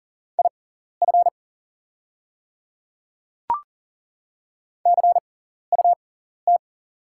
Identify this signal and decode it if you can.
Morse code, heard as a keyed tone, about 38 wpm, spaced as for 14 wpm, 690 Hz. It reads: IF CUT
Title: 6x Character Spacing - Top 300 Words - 40wpm